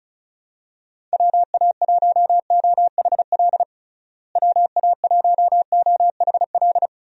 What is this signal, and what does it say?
Morse code, heard as a keyed tone, about 34 wpm, 700 Hz. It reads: WA1OHL WA1OHL
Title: USA Call Signs - 6 Characters - Repeated 2x - 35wpm